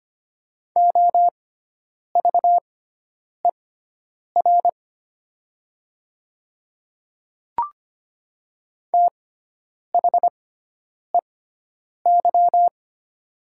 Morse code, heard as a keyed tone, about 25 wpm, spaced as for 9 wpm, 690 Hz. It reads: OVER THEY
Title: 6x Character Spacing - Top 100 Words - 25wpm